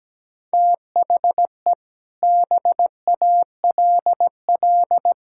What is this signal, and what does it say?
Morse code, heard as a keyed tone, about 17 wpm, 700 Hz. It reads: THE BALL
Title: Sentences from Top 1000 Words - Review Part 6 - 17wpm